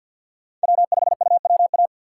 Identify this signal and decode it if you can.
Morse code, heard as a keyed tone, about 53 wpm, 700 Hz. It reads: W5UWA
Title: USA Call Signs - 5 Characters - 6x Character Spacing - 50wpm